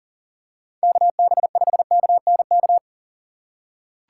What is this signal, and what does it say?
Morse code, heard as a keyed tone, about 37 wpm, 700 Hz. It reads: KB5KNK
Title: USA Call Signs - 6 Characters - 6x Character Spacing - 40wpm